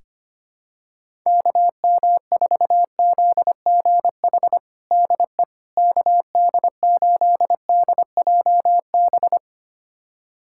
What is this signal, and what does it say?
Morse code, heard as a keyed tone, about 25 wpm, 700 Hz. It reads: KM4ZGH DE KD8DJB
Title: USA Callsign DE Callsign 25wpm